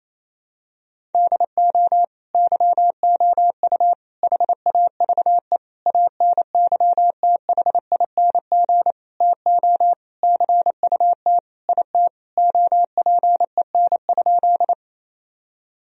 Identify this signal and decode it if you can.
Morse code, heard as a keyed tone, about 28 wpm, 700 Hz. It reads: DO YOU HAVE ANYTHING TO CUT IT OPEN?
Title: Sentences from Top 1000 Words - Review Part 1 - 28wpm